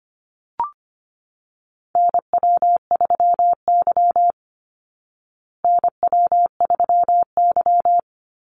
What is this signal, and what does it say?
Morse code, heard as a keyed tone, about 25 wpm, 700 Hz. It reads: NW3Y NW3Y
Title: Call Signs Repeated 2x - 25wpm